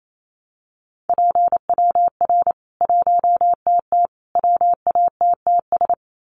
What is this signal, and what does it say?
Morse code, heard as a keyed tone, about 28 wpm, 700 Hz. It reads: PWR 1TT WATTS
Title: Common QSO Phrases 28wpm